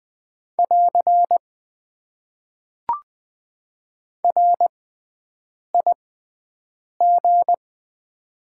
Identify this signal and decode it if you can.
Morse code, heard as a keyed tone, about 20 wpm, spaced as for 7 wpm, 700 Hz. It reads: <AR> RIG
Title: Top 90 QSO Elements - 6x Character Spacing - 20wpm